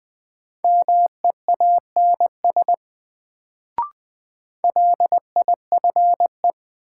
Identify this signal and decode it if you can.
Morse code, heard as a keyed tone, about 20 wpm, 700 Hz. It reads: MEANS LIFE
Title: Sets of 5 Words - New Word Drill - 20wpm